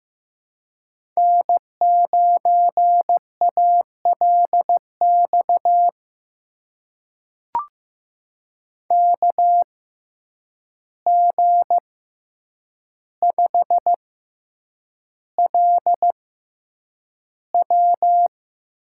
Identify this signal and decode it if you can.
Morse code, heard as a keyed tone, about 15 wpm, spaced as for 8 wpm, 700 Hz. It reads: N9ALX KG5LW
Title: USA Call Signs - 5 Characters - 6x Character Spacing - 15wpm